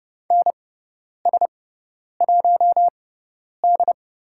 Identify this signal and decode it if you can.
Morse code, heard as a keyed tone, about 30 wpm, spaced as for 10 wpm, 700 Hz. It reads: NS1D
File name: Lesson 17g - Letter u - Call Signs 30wpm